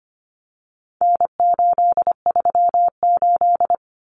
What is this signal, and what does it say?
Morse code, heard as a keyed tone, about 25 wpm, 700 Hz. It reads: N838
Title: Common Cut Numbers - 4-Digit 25wpm